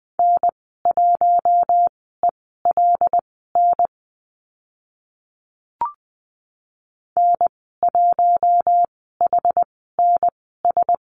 Morse code, heard as a keyed tone, about 20 wpm, spaced as for 14 wpm, 700 Hz. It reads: N1ELN N1HNS
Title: 076 Lesson 13g - Letter L - Call Signs 20wpm